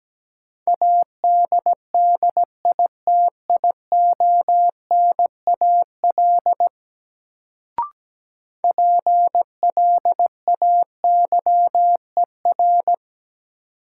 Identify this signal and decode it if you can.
Morse code, heard as a keyed tone, about 17 wpm, 700 Hz. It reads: ADDITIONAL PLAYER
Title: Sentences from Top 1000 Words - Review Part 4 - New Word Drill - 17wpm